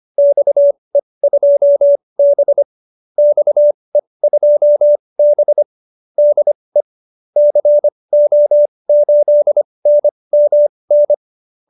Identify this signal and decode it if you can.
Morse code, heard as a keyed tone, about 25 wpm, spaced as for 20 wpm, 570 Hz. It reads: XE2B XE2B DE CO8NMN